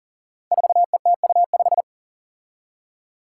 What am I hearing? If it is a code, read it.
Morse code, heard as a keyed tone, about 40 wpm, 700 Hz. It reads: 4ETU5
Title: Five Challenging Letter-Number Combinations 40wpm